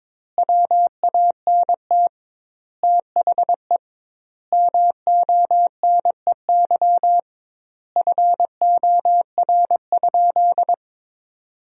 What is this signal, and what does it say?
Morse code, heard as a keyed tone, about 22 wpm, 700 Hz. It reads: WANT THE MONEY FOR?
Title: 2x Word-Spacing - Sentences from Top 400 Words - 22wpm